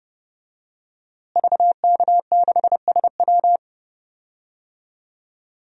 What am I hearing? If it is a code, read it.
Morse code, heard as a keyed tone, about 30 wpm, 700 Hz. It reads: VK6SW
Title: DXCC-100 Call Signs - 5 Characters - Repeated 3x - 30wpm